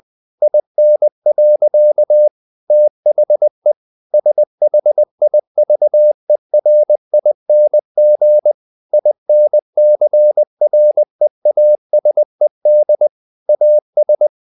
Morse code, heard as a keyed tone, about 20 wpm, 600 Hz. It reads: IN. THE SHIVERING INCREASED AS
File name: the old man and the sea / the old man and the sea part 1 20wpm